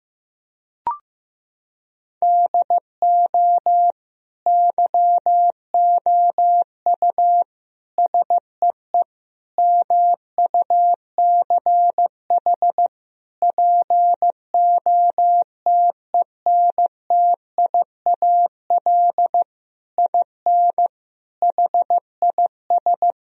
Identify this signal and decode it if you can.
Morse code, heard as a keyed tone, about 15 wpm, 700 Hz. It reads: DO YOU SEE MUCH POTENTIAL IN HIS
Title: Sentences from Top 1000 Words - Review Part 4 - 15wpm